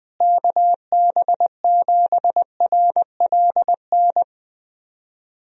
Morse code, heard as a keyed tone, about 20 wpm, 700 Hz. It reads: KB7RLN
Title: USA Call Signs - 6 Characters - Repeated 2x - 20wpm